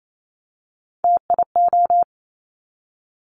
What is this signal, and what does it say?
Morse code, heard as a keyed tone, about 28 wpm, 700 Hz. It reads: TIO
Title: Top Three Letter Combinations in Words 28wpm